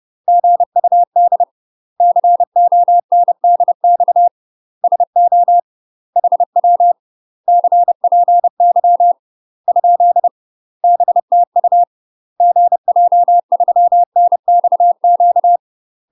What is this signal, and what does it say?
Morse code, heard as a keyed tone, about 30 wpm, 700 Hz. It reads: GUD CONDX SO HW CPY ? BTU GJ3NXQ
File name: UK MASTER QSOs TRAINING / 30WPM / UK MASTER QSOs 0001